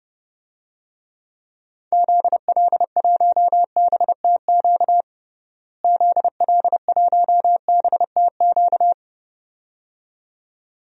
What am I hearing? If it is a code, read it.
Morse code, heard as a keyed tone, about 32 wpm, 700 Hz. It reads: ZL1BTQ ZL1BTQ
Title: DXCC-100 Call Signs - 6 Characters - Repeated 2x - 30wpm